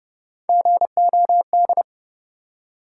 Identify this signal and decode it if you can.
Morse code, heard as a keyed tone, about 32 wpm, 700 Hz. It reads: GOD
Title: Three Letter Words - 30wpm